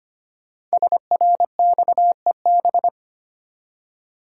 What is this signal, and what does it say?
Morse code, heard as a keyed tone, about 25 wpm, 700 Hz. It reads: SRXEB
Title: Code Groups - 5 Letters - 25wpm